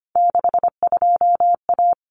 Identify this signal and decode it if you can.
Morse code, heard as a keyed tone, about 25 wpm, 700 Hz. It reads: 62A
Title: Common Cut Numbers - 3-Digit 25wpm